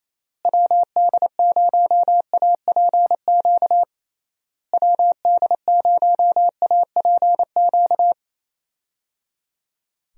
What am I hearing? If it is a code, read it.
Morse code, heard as a keyed tone, about 27 wpm, 700 Hz. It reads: WD0APQ WD0APQ
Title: USA Call Signs - 6 Characters - Repeated 3x - 28wpm